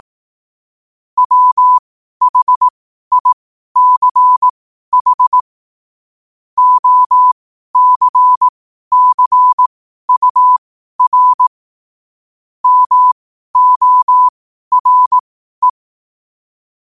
Morse code, heard as a keyed tone, about 18 wpm, spaced as for 12 wpm, 1000 Hz. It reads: WHICH OCCUR MORE
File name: qotdfiles / 12wpm